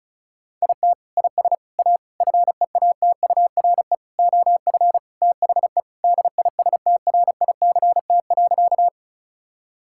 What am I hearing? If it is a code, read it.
Morse code, heard as a keyed tone, about 36 wpm, 700 Hz. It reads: IT IS A FEATURE OF THE DISTRICT.
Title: Sentences from Top 1600 Words 35wpm